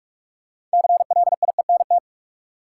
Morse code, heard as a keyed tone, about 47 wpm, 700 Hz. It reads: CLIENT
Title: Six Letter Words - 45wpm